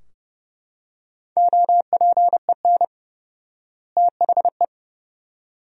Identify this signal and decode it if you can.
Morse code, heard as a keyed tone, about 30 wpm, 700 Hz. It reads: OPEN THE
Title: 4x Word-Spacing - Sentences from Top 300 Words - 30wpm